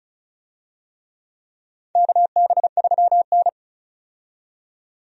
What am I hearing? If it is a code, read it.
Morse code, heard as a keyed tone, about 36 wpm, 700 Hz. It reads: KB3N
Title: USA Call Signs - 4 Characters - 35wpm